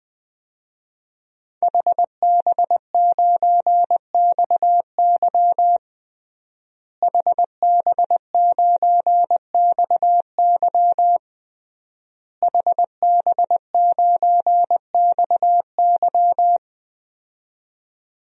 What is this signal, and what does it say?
Morse code, heard as a keyed tone, about 21 wpm, 700 Hz. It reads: HB9XY HB9XY HB9XY
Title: DXCC-100 Call Signs - 5 Characters - Repeated 3x - 20wpm